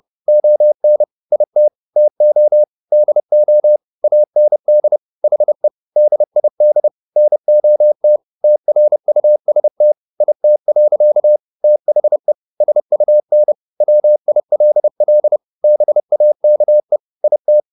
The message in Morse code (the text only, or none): ON IT TO DO AND HE DID NOT TRUST IT. THE SUN WILL BAKE IT